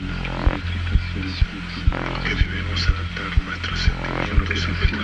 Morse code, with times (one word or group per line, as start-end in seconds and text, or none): none